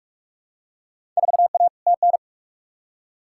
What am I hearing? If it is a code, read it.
Morse code, heard as a keyed tone, about 46 wpm, 700 Hz. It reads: 4A TN
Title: Field Day Exchanges - 45wpm